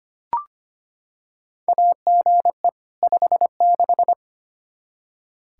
Morse code, heard as a keyed tone, about 25 wpm, 700 Hz. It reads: AGE 56
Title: K1USN SST QSO Practice 25wpm